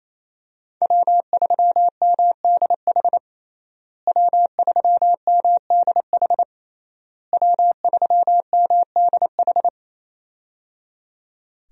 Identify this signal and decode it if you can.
Morse code, heard as a keyed tone, about 28 wpm, 700 Hz. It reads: W3MDH W3MDH W3MDH
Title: USA Call Signs - 5 Characters - Repeated 3x - 28wpm